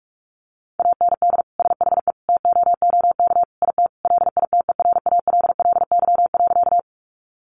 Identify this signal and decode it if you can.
Morse code, heard as a keyed tone, about 44 wpm, 700 Hz. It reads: AND SHE TOOK IT LITERALLY.
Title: Sentences from Top 1400 Words 45wpm